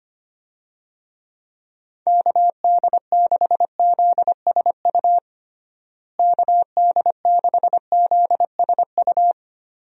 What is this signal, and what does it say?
Morse code, heard as a keyed tone, about 25 wpm, 700 Hz. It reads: KD6ZSU KD6ZSU
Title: USA Call Signs - 6 Characters - Repeated 2x - 25wpm